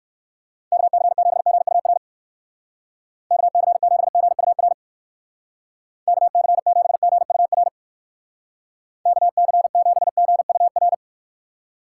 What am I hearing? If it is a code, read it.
Morse code, heard as a keyed tone, about 55 wpm, 700 Hz. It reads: KC7GUR KC7GUR KC7GUR KC7GUR
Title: Speed Racing - USA 2x3 Call Signs 45wpm